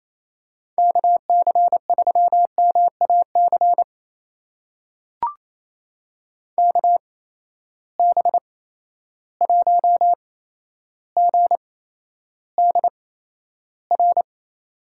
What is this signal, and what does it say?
Morse code, heard as a keyed tone, about 28 wpm, spaced as for 12 wpm, 700 Hz. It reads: KC3MAC KB1GDR
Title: USA Call Signs - 8x Character Spacing - 28wpm